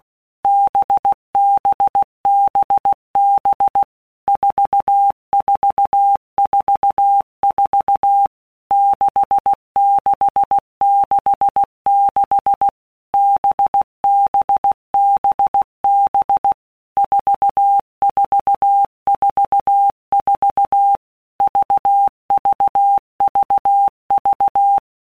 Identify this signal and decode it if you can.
Morse code, heard as a keyed tone, about 16 wpm, 800 Hz. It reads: BBBB 4444 6666 BBBB 4444 VVVV
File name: Basic Letter Groups By WPM's / Groups@15 / VB46@15